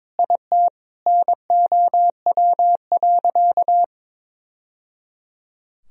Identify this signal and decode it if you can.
Morse code, heard as a keyed tone, about 22 wpm, 700 Hz. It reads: IT NOW.
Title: Sentences from Top 1000 Words - Review Part 4 - 22wpm